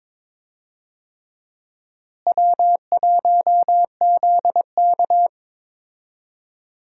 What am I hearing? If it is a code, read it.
Morse code, heard as a keyed tone, about 22 wpm, 700 Hz. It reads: W1ZK
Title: USA Call Signs - 4 Characters - Repeated 3x - 22wpm